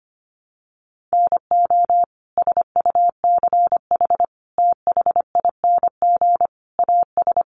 Morse code, heard as a keyed tone, about 25 wpm, 700 Hz. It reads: NO SUCH THING AS
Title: Sets of 5 Words Encore 25wpm